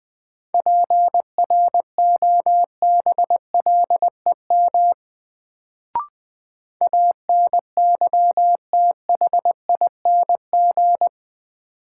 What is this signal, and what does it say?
Morse code, heard as a keyed tone, about 20 wpm, 700 Hz. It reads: PROBLEM ANYTHING
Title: Sentences from Top 1000 Words - New Word Drill - 20wpm